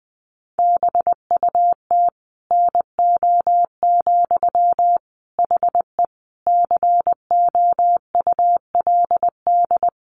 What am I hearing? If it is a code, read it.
Morse code, heard as a keyed tone, about 20 wpm, 700 Hz. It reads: BUT NO, HE COULD
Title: Sentences from Top 1400 Words 20wpm